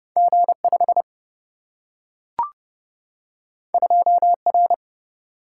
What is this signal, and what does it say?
Morse code, heard as a keyed tone, about 30 wpm, 700 Hz. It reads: G5 2R
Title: Single-Letter Single-Number 30wpm